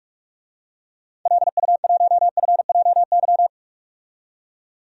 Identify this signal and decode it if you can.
Morse code, heard as a keyed tone, about 45 wpm, 700 Hz. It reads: LU1FJY
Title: DXCC-100 Call Signs - 6 Characters - 45wpm